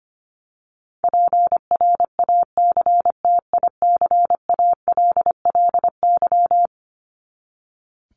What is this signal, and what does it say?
Morse code, heard as a keyed tone, about 25 wpm, 700 Hz. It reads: PRACTICALLY
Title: LY Suffix - Top 100 Words with Base Words 25wpm